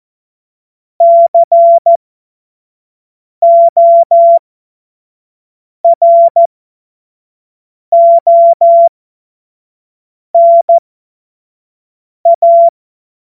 Morse code, heard as a keyed tone, about 14 wpm, spaced as for 5 wpm, 680 Hz. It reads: CORONA
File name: Morsekurs / Witze / Witz 06. Juli 2021, 25 F70